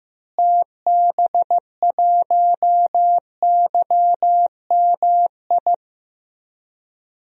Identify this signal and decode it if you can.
Morse code, heard as a keyed tone, about 16 wpm, 700 Hz. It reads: TB1YMI
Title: DXCC-100 Call Signs - 6 Characters - Repeated 2x - 15wpm